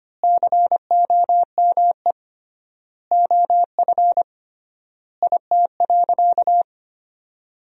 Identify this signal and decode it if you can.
Morse code, heard as a keyed tone, about 25 wpm, 700 Hz. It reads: COME OF IT.
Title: 3x Word-Spacing - Sentences from Top 500 Words - 25wpm